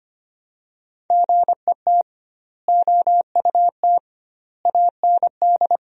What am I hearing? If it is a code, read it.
Morse code, heard as a keyed tone, about 25 wpm, 700 Hz. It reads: GET OUT AND
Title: 2x Word-Spacing - Sentences from Top 200 Words - 25wpm